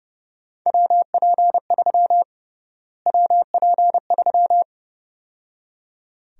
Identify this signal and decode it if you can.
Morse code, heard as a keyed tone, about 31 wpm, 700 Hz. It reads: WP3 WP3
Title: USA Call Sign Prefixes - Repeated 2x - 30wpm